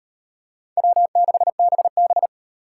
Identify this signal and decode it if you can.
Morse code, heard as a keyed tone, about 38 wpm, 700 Hz. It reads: W6BB